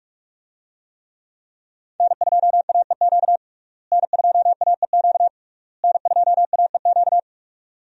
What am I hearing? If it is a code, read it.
Morse code, heard as a keyed tone, about 41 wpm, 700 Hz. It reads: N2AEQ N2AEQ N2AEQ
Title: USA Call Signs - 5 Characters - Repeated 3x - 45wpm